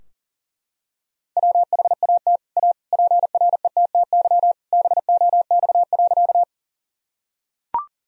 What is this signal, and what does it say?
Morse code, heard as a keyed tone, about 37 wpm, 700 Hz. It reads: WHAT A PRETTY BOX.
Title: Sentences from Top 1000 Words - Review Part 2 - 40wpm